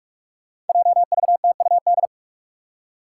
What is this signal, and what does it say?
Morse code, heard as a keyed tone, about 41 wpm, 700 Hz. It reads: JVTUD
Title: Five Challenging Letter-Number Combinations 45wpm